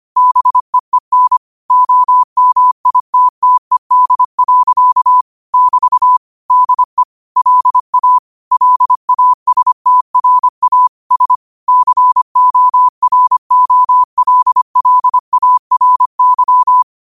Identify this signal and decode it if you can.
Morse code, heard as a keyed tone, about 25 wpm, 1000 Hz. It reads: BEEN OMITTED. <BT> DE LA LASTRA S COROLLARY
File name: qotdfiles / 25wpm